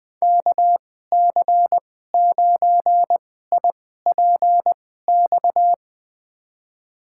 Morse code, heard as a keyed tone, about 20 wpm, spaced as for 14 wpm, 700 Hz. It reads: KC9IPX
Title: USA Call Signs - 6 Characters - 2x Character Spacing - 20wpm